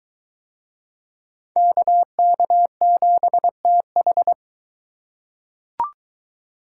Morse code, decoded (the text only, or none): KK7TH